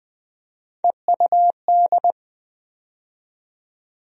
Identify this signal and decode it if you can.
Morse code, heard as a keyed tone, about 20 wpm, 700 Hz. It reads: EUD
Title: USA Call Sign Postfixes - 3 Letters 20wpm